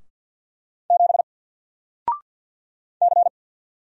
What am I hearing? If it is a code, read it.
Morse code, heard as a keyed tone, about 50 wpm, 700 Hz. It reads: 7 C
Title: Single Letter-Number 50wpm